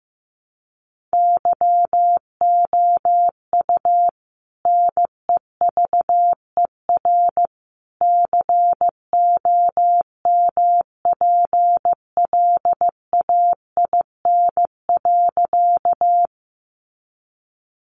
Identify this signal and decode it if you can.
Morse code, heard as a keyed tone, about 15 wpm, 700 Hz. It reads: YOU NEVER COMPLAIN.
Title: Sentences from Top 1300 Words 15wpm